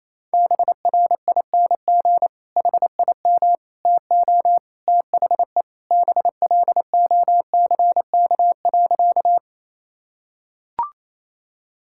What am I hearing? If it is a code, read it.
Morse code, heard as a keyed tone, about 28 wpm, 700 Hz. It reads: BRING HIM TO THE BLOCK.
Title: Sentences from Top 1600 Words 28wpm